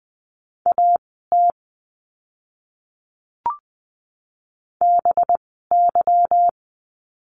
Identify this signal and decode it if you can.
Morse code, heard as a keyed tone, about 20 wpm, spaced as for 15 wpm, 700 Hz. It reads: AT BY
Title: 185 Lesson 30f - Letter b - Words 20wpm